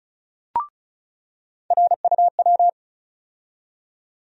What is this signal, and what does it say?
Morse code, heard as a keyed tone, about 34 wpm, 700 Hz. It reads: RUW
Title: USA Call Sign Postfixes - 3 Letters 35wpm